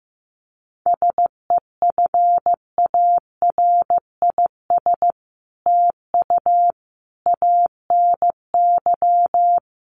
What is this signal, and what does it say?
Morse code, heard as a keyed tone, about 15 wpm, 700 Hz. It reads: SEFARIS TU ANY
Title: A Sound Of Thunder 15wpm